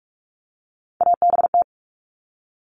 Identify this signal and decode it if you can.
Morse code, heard as a keyed tone, about 44 wpm, 700 Hz. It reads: ABT
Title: Abbreviations 45wpm